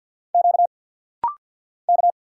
Morse code, heard as a keyed tone, about 48 wpm, 690 Hz. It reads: Q K
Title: Single Letters - Rapid-Fire 50wpm